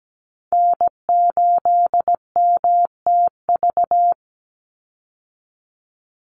Morse code, heard as a keyed tone, about 17 wpm, 700 Hz. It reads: N8MTV
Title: Call Signs - 17wpm